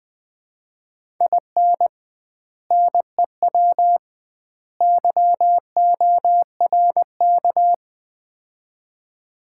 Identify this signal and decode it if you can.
Morse code, heard as a keyed tone, about 20 wpm, 700 Hz. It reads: IN NEW YORK
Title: 2x Word-Spacing - Sets of 3 Words - 20wpm